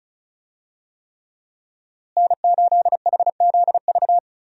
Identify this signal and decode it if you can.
Morse code, heard as a keyed tone, about 36 wpm, 700 Hz. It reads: N8HZV
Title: USA Call Signs - 5 Characters - Repeated 3x - 35wpm